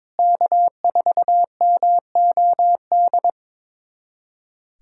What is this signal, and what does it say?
Morse code, heard as a keyed tone, about 23 wpm, 700 Hz. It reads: K4MOD
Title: USA Callsign DE Callsign - 5 Characters - 22wpm